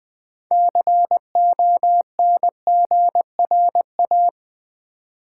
Morse code, heard as a keyed tone, about 20 wpm, 700 Hz. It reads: CONGRA
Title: Build A Word - 6 Letters At A Time - 12-Letter Words - 20wpm